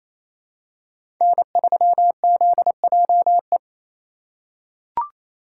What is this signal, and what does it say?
Morse code, heard as a keyed tone, about 28 wpm, 700 Hz. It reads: N3ZJE
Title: USA Call Signs - 5 Characters - 2x Character Spacing - 28wpm